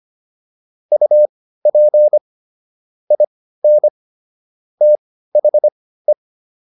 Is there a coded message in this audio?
Yes: Morse code from 0.9 to 6.1 s, heard as a keyed tone, about 25 wpm, spaced as for 15 wpm, 600 Hz. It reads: UP IN THE